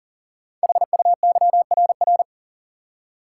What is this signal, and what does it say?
Morse code, heard as a keyed tone, about 42 wpm, 700 Hz. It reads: HUYRR
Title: Code Groups - 5 Letters - 40wpm